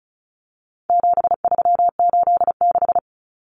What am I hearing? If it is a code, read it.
Morse code, heard as a keyed tone, about 34 wpm, 700 Hz. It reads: 7386
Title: Common Cut Numbers - 4-Digit 35wpm